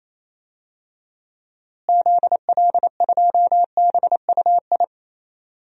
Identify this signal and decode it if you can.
Morse code, heard as a keyed tone, about 27 wpm, 700 Hz. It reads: ZL2BUI
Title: DXCC-100 Call Signs - 6 Characters - Repeated 2x - 28wpm